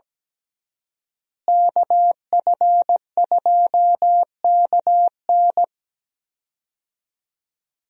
Morse code, heard as a keyed tone, about 17 wpm, 700 Hz. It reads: KF2KN
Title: USA Call Signs - 5 Characters - Repeated 2x - 17wpm